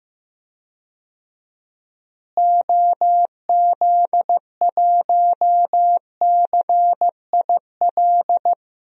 Morse code, heard as a keyed tone, about 15 wpm, 700 Hz. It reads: OZ1CIL
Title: DXCC-100 Call Signs - 6 Characters - Repeated 2x - 15wpm